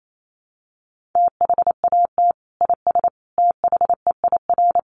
Morse code, heard as a keyed tone, about 28 wpm, 700 Hz. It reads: THAT IS THEIR
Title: Sentences from Top 1400 Words 28wpm